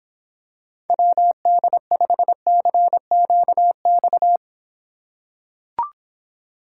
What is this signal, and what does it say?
Morse code, heard as a keyed tone, about 26 wpm, 700 Hz. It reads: WD5CQX